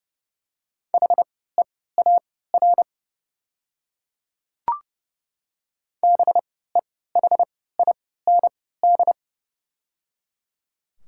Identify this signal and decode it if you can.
Morse code, heard as a keyed tone, about 30 wpm, spaced as for 17 wpm, 700 Hz. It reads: HEAR BEHIND
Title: 3x Character Spacing - Top 500 Words - 30wpm